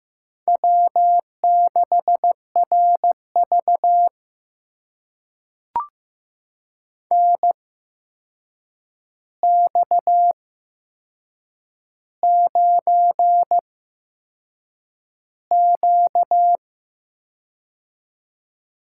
Morse code, heard as a keyed tone, about 15 wpm, spaced as for 6 wpm, 700 Hz. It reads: W6RV NX9Q
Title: USA Call Signs - 4 Characters - 8x Character Spacing - 15wpm